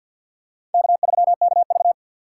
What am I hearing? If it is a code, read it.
Morse code, heard as a keyed tone, about 47 wpm, 690 Hz. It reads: K3KV
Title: USA Call Signs - 2x Character Spacing - 50wpm